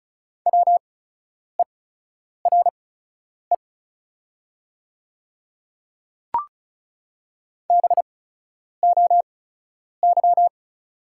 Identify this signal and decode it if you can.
Morse code, heard as a keyed tone, about 36 wpm, spaced as for 10 wpm, 690 Hz. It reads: WERE BOY
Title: 8x Character Spacing - Top 200 Words - 35wpm